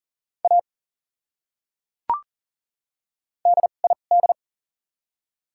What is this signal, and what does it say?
Morse code, heard as a keyed tone, about 40 wpm, spaced as for 6 wpm, 700 Hz. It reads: AD ID
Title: 2x Character Spacing - Top 100 Words - 40wpm